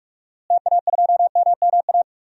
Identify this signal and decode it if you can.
Morse code, heard as a keyed tone, about 42 wpm, 700 Hz. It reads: TA2MMA
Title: DXCC-100 Call Signs - 6 Characters - Repeated 2x - 45wpm